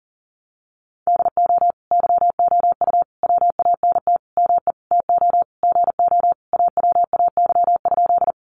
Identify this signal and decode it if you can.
Morse code, heard as a keyed tone, about 40 wpm, 700 Hz. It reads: DO YOU WANT ME TO GO AWAY?
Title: Sentences from Top 600 Words 40wpm